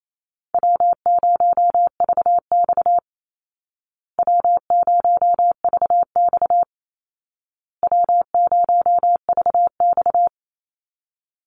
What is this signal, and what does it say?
Morse code, heard as a keyed tone, about 28 wpm, 700 Hz. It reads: W0VX W0VX W0VX
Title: Easy Call Signs Repeated 3x - 28wpm